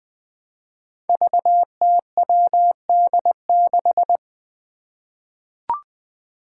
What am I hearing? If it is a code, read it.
Morse code, heard as a keyed tone, about 20 wpm, 700 Hz. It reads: VTWD6
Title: Five Challenging Letter-Number Combinations 20wpm